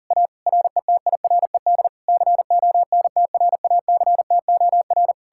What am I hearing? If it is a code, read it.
Morse code, heard as a keyed tone, about 52 wpm, 700 Hz. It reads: A RETIRED CONTRACTOR